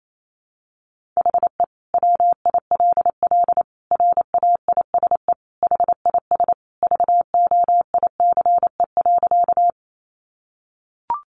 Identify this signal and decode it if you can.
Morse code, heard as a keyed tone, about 28 wpm, 700 Hz. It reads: HE WILL RAISE HIS VOICE.
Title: Sentences from Top 1400 Words 28wpm